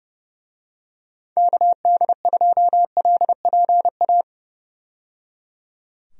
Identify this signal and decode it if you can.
Morse code, heard as a keyed tone, about 30 wpm, 700 Hz. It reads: KD2LPA
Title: USA Call Signs - 6 Characters - 30wpm